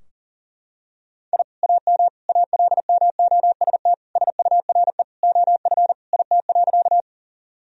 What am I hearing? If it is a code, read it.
Morse code, heard as a keyed tone, about 37 wpm, 700 Hz. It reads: I AM ALMOST SURE OF IT.
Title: Sentences from Top 1000 Words - Review Part 4 - 40wpm